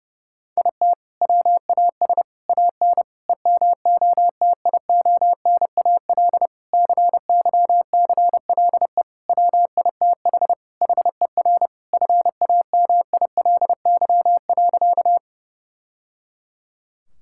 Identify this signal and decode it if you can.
Morse code, heard as a keyed tone, about 30 wpm, 700 Hz. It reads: IT WAS AN EMOTIONAL CYCLE WITH HER FAMILY.
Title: Sentences from Top 1000 Words - Review Part 6 - 30wpm